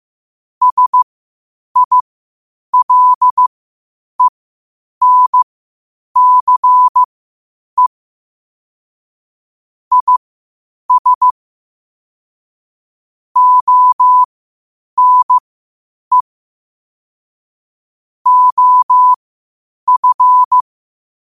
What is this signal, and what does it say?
Morse code, heard as a keyed tone, about 15 wpm, spaced as for 8 wpm, 1000 Hz. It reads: SILENCE IS ONE OF
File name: qotdfiles / 8wpm